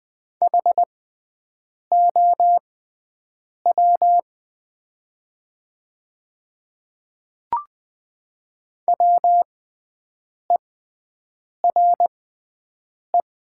Morse code, heard as a keyed tone, about 20 wpm, spaced as for 7 wpm, 700 Hz. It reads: HOW WERE
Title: 6x Character Spacing - Top 100 Words - 20wpm